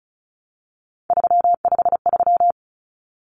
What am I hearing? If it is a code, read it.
Morse code, heard as a keyed tone, about 34 wpm, 700 Hz. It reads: 353
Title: Numbers - 3-Digit 35wpm